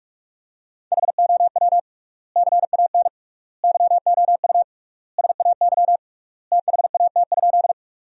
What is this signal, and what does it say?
Morse code, heard as a keyed tone, about 46 wpm, 700 Hz. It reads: HOW CAN YOU SAY THAT?
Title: 3x Word-Spacing - Sentences from Top 200 Words - 45wpm